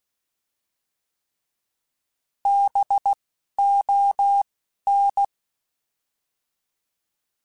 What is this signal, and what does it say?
Morse code, heard as a keyed tone, about 16 wpm, spaced as for 11 wpm, 800 Hz. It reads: BON
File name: Alphabetical / Alphabetical mp3s / Bs @10